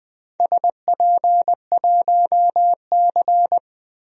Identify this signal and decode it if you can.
Morse code, heard as a keyed tone, about 20 wpm, 700 Hz. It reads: SP1C
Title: DXCC-100 Call Signs - 4 Characters - 20wpm